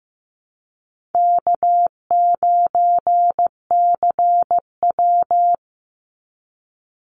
Morse code, heard as a keyed tone, about 15 wpm, 700 Hz. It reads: K9CW
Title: Easy Call Signs - 15wpm